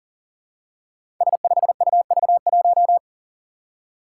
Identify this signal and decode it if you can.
Morse code, heard as a keyed tone, about 40 wpm, 700 Hz. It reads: S5UV1
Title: Five Challenging Letter-Number Combinations 40wpm